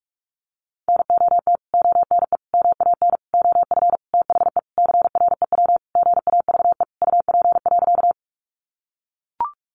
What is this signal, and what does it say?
Morse code, heard as a keyed tone, about 44 wpm, 700 Hz. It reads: NOT ONE MAN OF THE CREW GAVE UP.
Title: Sentences from Top 1500 Words 45wpm